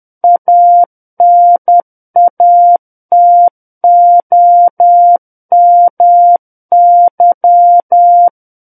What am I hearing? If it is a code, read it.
Morse code, heard as a keyed tone, about 10 wpm, 700 Hz. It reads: ANATOMY
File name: Book One / 01-Chapter 1 - 10 WPM